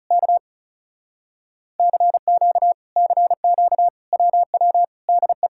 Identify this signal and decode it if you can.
Morse code, heard as a keyed tone, about 35 wpm, 700 Hz. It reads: K CQ CQ WW DE
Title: Speed-Racing Common QSO Phrases - 30wpm